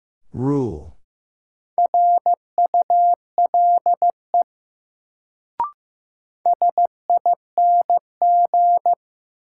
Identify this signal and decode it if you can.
Morse code, heard as a keyed tone, about 15 wpm, 700 Hz. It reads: RULE SING
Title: Sentences from Top 400 Words - New Word Drill - 15wpm